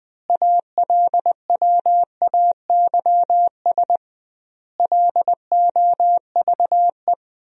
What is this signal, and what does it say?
Morse code, heard as a keyed tone, about 20 wpm, 700 Hz. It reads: ALWAYS LOVE
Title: 2x Word-Spacing - Sentences from Top 400 Words - 20wpm